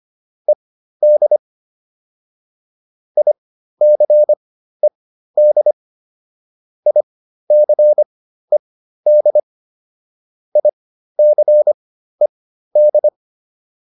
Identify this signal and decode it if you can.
Morse code, heard as a keyed tone, about 25 wpm, spaced as for 13 wpm, 600 Hz. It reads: ED ICED ICED ICED